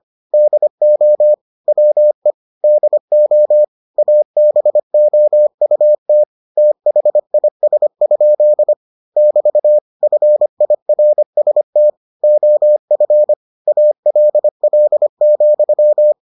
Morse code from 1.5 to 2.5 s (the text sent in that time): WE